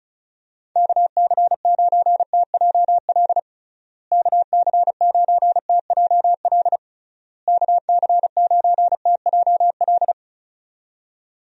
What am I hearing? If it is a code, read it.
Morse code, heard as a keyed tone, about 36 wpm, 700 Hz. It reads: KC9TJL KC9TJL KC9TJL
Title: USA Call Signs - 6 Characters - Repeated 3x - 35wpm